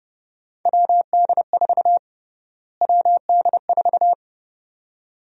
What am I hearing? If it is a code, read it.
Morse code, heard as a keyed tone, about 30 wpm, 700 Hz. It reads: WD4 WD4
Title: USA Call Sign Prefixes - Repeated 2x - 30wpm